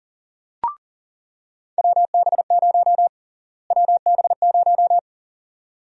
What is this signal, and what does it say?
Morse code, heard as a keyed tone, about 40 wpm, 700 Hz. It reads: WB0 WB0
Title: USA Call Sign Prefixes - Repeated 2x - 40wpm